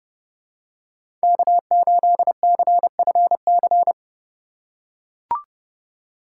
Morse code, heard as a keyed tone, about 32 wpm, 700 Hz. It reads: K8CFC
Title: USA Call Signs - 5 Characters - 4x Character Spacing - 30wpm